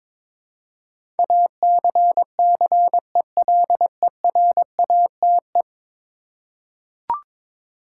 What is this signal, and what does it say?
Morse code, heard as a keyed tone, about 22 wpm, 700 Hz. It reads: ACCELERATE
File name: practice / 22wpm / 10 Letter Words - First 3 Letters Given - 22wpm